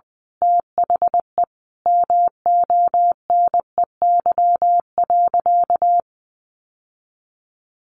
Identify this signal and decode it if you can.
Morse code, heard as a keyed tone, about 20 wpm, 700 Hz. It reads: THE MONEY.
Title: Sentences from Top 1100 Words 20wpm